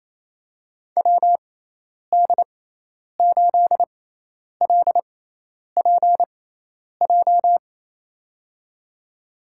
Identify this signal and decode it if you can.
Morse code, heard as a keyed tone, about 27 wpm, spaced as for 10 wpm, 700 Hz. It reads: WD8LPJ
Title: USA Call Signs - 6x Character Spacing - 28wpm